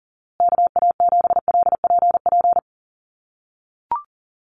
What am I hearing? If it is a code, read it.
Morse code, heard as a keyed tone, about 40 wpm, 700 Hz. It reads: KA7LPP
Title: USA Call Signs Repeated 2x 40wpm